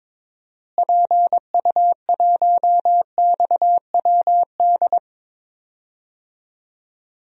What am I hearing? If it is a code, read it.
Morse code, heard as a keyed tone, about 22 wpm, 700 Hz. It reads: PU1XWD
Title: DXCC-100 Call Signs - 6 Characters - Repeated 2x - 22wpm